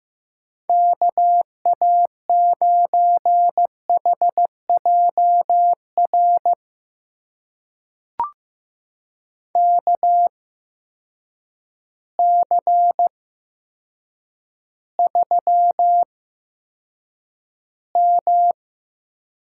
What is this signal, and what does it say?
Morse code, heard as a keyed tone, about 15 wpm, spaced as for 8 wpm, 700 Hz. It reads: KA9HJR KC3M
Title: USA Call Signs - 6 Characters - 8x Character Spacing - 15wpm